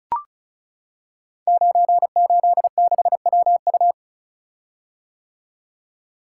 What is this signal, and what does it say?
Morse code, heard as a keyed tone, about 36 wpm, 700 Hz. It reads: 986WU
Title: Code Groups - 5 Letter:Numbers - 35wpm